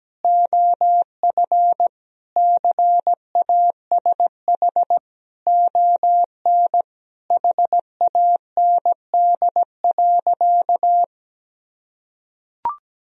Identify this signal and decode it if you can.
Morse code, heard as a keyed tone, about 17 wpm, 700 Hz. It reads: OF CASH ON HAND.
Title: Sentences from Top 1000 Words - Review Part 4 - 17wpm